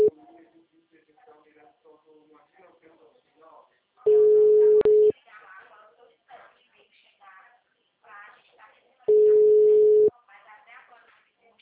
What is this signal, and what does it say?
Ringback tone (Europe, 425 Hz), with speech.
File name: out- -2046- - - .14509